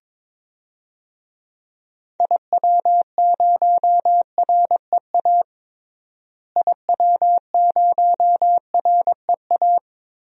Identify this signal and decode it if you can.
Morse code, heard as a keyed tone, about 21 wpm, 700 Hz. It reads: IW0REA IW0REA
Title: DXCC-100 Call Signs - 6 Characters - Repeated 2x - 22wpm